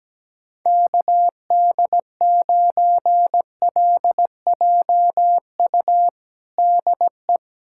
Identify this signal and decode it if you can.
Morse code, heard as a keyed tone, about 17 wpm, 700 Hz. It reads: KD9LJU DE
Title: USA Callsign DE Callsign 17wpm